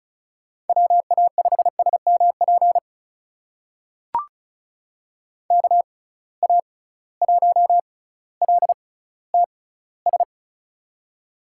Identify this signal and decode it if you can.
Morse code, heard as a keyed tone, about 36 wpm, spaced as for 18 wpm, 690 Hz. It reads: WA5SMP KA1LTS
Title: USA Call Signs - 6 Characters - 6x Character Spacing - 35wpm